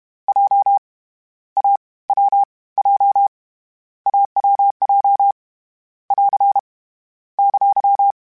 Morse code, heard as a keyed tone, about 31 wpm, 800 Hz. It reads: J A W J AWJ <AR> !
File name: Short Runs / Short Runs-mp3s / AWJ @30